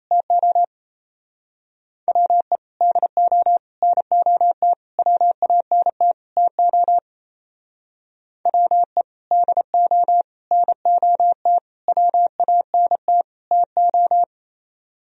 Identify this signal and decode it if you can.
Morse code, heard as a keyed tone, about 35 wpm, 700 Hz. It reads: TO WE DO NOT WANT TO WE DO NOT WANT TO
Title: Speed-Racing - Sets of 5 Words 28wpm